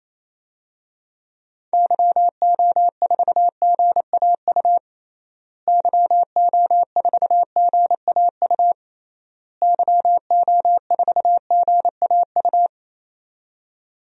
Morse code, heard as a keyed tone, about 28 wpm, 700 Hz. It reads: YO4GAU YO4GAU YO4GAU
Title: DXCC-100 Call Signs - 6 Characters - Repeated 3x - 28wpm